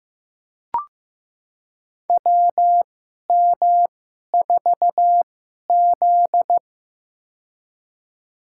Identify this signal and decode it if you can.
Morse code, heard as a keyed tone, about 15 wpm, spaced as for 11 wpm, 700 Hz. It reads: WM4Z
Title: USA Call Signs - 4 Characters - 2x Character Spacing - 15wpm